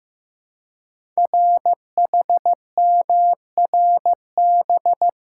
Morse code, heard as a keyed tone, about 15 wpm, 700 Hz. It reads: RHMRB
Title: Code Groups - 5 Letters - 15wpm